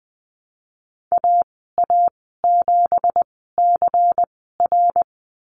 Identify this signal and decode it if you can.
Morse code, heard as a keyed tone, about 20 wpm, spaced as for 15 wpm, 700 Hz. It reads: AA7CR
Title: 167 Lesson 27g - Number 7 - Call Signs 20wpm